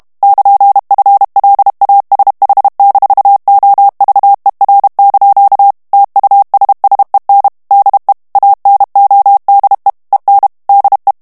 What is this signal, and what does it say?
Morse code, heard as a keyed tone, about 32 wpm, 800 Hz. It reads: <KN>FLASH-OVER) TUSSEN DE ANODE EN DE
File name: morse mp3 / HOLLANDAIS texte clair 2 vitesse 30 mots minute